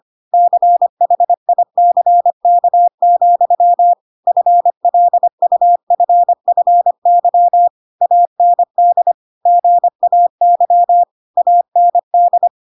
Morse code, heard as a keyed tone, about 25 wpm, 700 Hz. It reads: CHICK, FLUFFY AND GAY AND
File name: Pinocchio / Pinocchio ch5 25 25 ik1hge